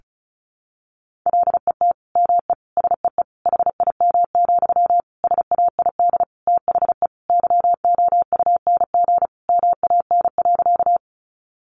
Morse code, heard as a keyed tone, about 34 wpm, 700 Hz. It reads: LET ME SEE HIM, SAID THE YOUNG MAN.
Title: Sentences from Top 400 Words 35wpm